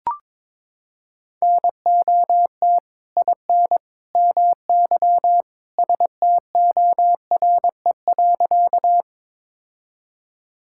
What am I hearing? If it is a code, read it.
Morse code, heard as a keyed tone, about 21 wpm, 700 Hz. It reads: NOT IN MY STORE.
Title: Sentences from Top 1000 Words - Review Part 5 - 22wpm